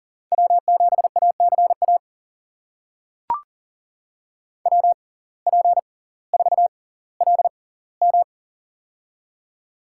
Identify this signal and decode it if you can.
Morse code, heard as a keyed tone, about 39 wpm, spaced as for 20 wpm, 700 Hz. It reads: W7ACA WP4LM
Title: USA Call Signs - 5 Characters - 6x Character Spacing - 40wpm